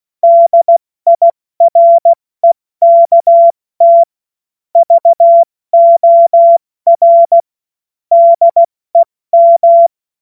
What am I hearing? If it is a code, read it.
Morse code, heard as a keyed tone, about 16 wpm, 680 Hz. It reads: DIREKT VOR DEM
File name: Morsekurs/Witze